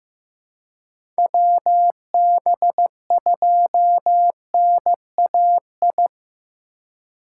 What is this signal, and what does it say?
Morse code, heard as a keyed tone, about 15 wpm, 700 Hz. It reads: WB2NAI
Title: USA Call Signs - 6 Characters - 2x Character Spacing - 15wpm